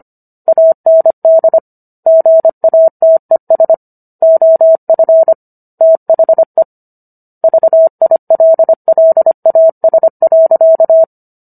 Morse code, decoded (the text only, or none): AND GATES OF THE VILLAS.